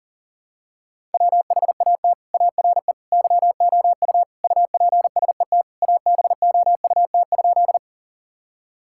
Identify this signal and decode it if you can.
Morse code, heard as a keyed tone, about 40 wpm, 700 Hz. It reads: WHAT ARE YOU UPSET ABOUT?
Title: Sentences from Top 1000 Words - Review Part 5 - 40wpm